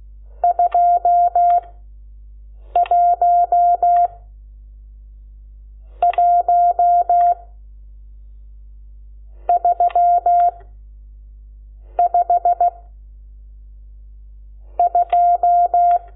Morse code, heard as a keyed tone, about 16 wpm, 700 Hz. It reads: 2 1 1 3 5 2
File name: Lecciones / CLASE Nº12